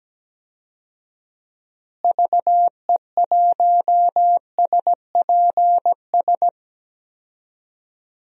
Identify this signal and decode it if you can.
Morse code, heard as a keyed tone, about 17 wpm, 700 Hz. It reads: VE1SPS
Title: DXCC-100 Call Signs - 6 Characters - Repeated 2x - 17wpm